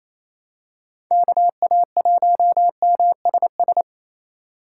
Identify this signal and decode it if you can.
Morse code, heard as a keyed tone, about 27 wpm, 700 Hz. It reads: KA1MSS
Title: USA Call Signs - 6x Character Spacing - 28wpm